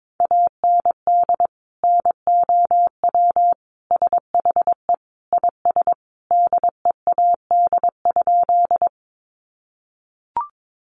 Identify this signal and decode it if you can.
Morse code, heard as a keyed tone, about 22 wpm, 700 Hz. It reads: AND NOW SHE IS DEAD?
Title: Sentences from Top 1500 Words 22wpm